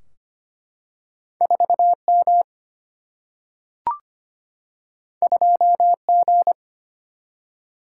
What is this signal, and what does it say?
Morse code, heard as a keyed tone, about 25 wpm, 700 Hz. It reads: 4M 2G
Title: Single-Letter Single-Number 25wpm